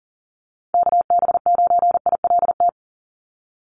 Morse code, heard as a keyed tone, about 42 wpm, 700 Hz. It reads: KB9ILT
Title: NA SOTA Activator Callsigns 40wpm